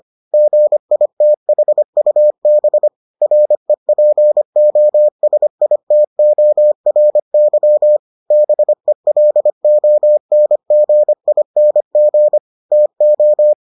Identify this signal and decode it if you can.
Morse code, heard as a keyed tone, about 25 wpm, 600 Hz. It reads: GITHUB REPOSITORY BELONGING TO